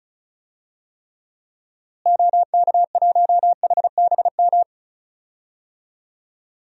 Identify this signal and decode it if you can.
Morse code, heard as a keyed tone, about 36 wpm, 700 Hz. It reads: OK1HBM
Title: DXCC-100 Call Signs - 6 Characters - Repeated 2x - 35wpm